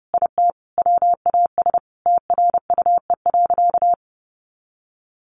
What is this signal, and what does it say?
Morse code, heard as a keyed tone, about 30 wpm, 700 Hz. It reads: IT WAS TRUE.
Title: Sentences from Top 400 Words 30wpm